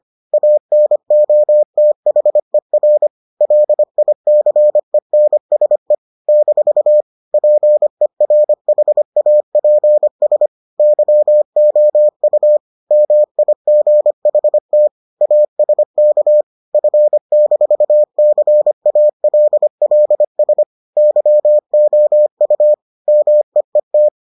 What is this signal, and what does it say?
Morse code, heard as a keyed tone, about 25 wpm, 600 Hz. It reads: ANOTHER LICENSE - PERHAPS YOU MIGHT ASK F-CALLS YOU MEET